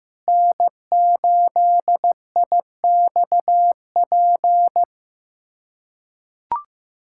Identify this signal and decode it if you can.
Morse code, heard as a keyed tone, about 16 wpm, 700 Hz. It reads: N8IXP
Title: USA Callsign DE Callsign - 5 Characters - 15wpm